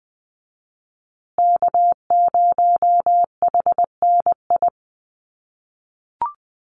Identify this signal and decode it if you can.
Morse code, heard as a keyed tone, about 20 wpm, 700 Hz. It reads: K0HNI
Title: USA Call Signs Repeated 3x 20wpm